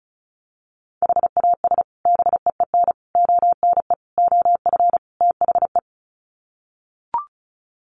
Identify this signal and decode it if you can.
Morse code, heard as a keyed tone, about 34 wpm, 700 Hz. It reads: HAS BEEN ONE OF THE
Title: Sets of 5 Words Encore 35wpm